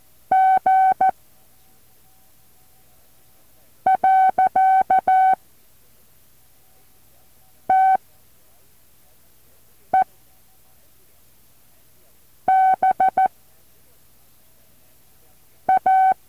Morse code, heard as a keyed tone, about 14 wpm, 760 Hz. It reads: G . T E B A
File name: intl morse code2